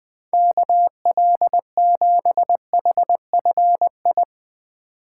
Morse code, heard as a keyed tone, about 20 wpm, 700 Hz. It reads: KL7HFI
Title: K1USN SST QSO Practice - 20wpm